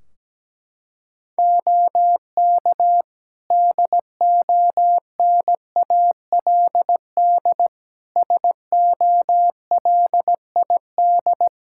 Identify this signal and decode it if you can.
Morse code, heard as a keyed tone, about 17 wpm, 700 Hz. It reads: OK DONALD SOLID
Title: Practice QSO - Exchange 3 - More Info - 17wpm